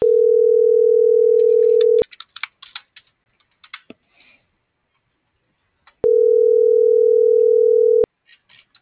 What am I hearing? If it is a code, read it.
Ringback tone (North America), 440 + 480 Hz.